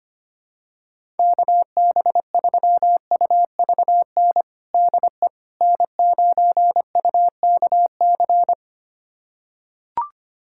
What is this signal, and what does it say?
Morse code, heard as a keyed tone, about 25 wpm, 700 Hz. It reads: KB3UVN DE N9UKC
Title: USA Callsign DE Callsign 25wpm